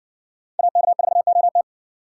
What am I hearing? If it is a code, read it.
Morse code, heard as a keyed tone, about 61 wpm, 700 Hz. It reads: AK4OT